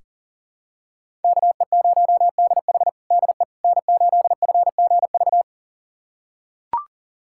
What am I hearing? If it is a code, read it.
Morse code, heard as a keyed tone, about 37 wpm, 700 Hz. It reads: KE0DH DE N8FGV
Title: USA Callsign DE Callsign - 5 Characters - 40wpm